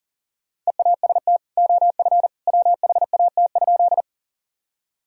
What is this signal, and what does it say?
Morse code, heard as a keyed tone, about 40 wpm, 700 Hz. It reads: EAST OF WHAT?
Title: Sentences from Top 1000 Words - Review Part 3 - 40wpm